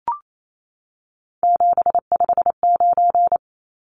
Morse code, heard as a keyed tone, about 28 wpm, 700 Hz. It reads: 759
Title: Numbers - 3-Digit 28wpm